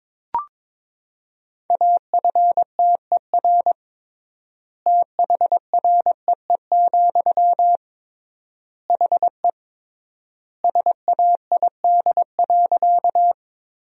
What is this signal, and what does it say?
Morse code, heard as a keyed tone, about 22 wpm, 700 Hz. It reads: AFTER THREE, HE SAID.
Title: 3x Word-Spacing - Sentences from Top 200 Words - 22wpm